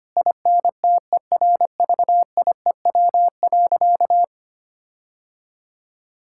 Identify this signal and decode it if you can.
Morse code, heard as a keyed tone, about 25 wpm, 700 Hz. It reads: INTERVIEW.
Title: Sentences from Top 1600 Words 25wpm